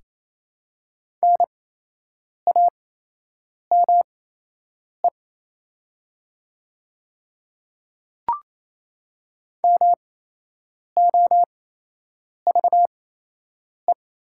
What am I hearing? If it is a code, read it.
Morse code, heard as a keyed tone, about 28 wpm, spaced as for 8 wpm, 700 Hz. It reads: NAME MOVE
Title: 8x Character Spacing - Top 300 Words - 28wpm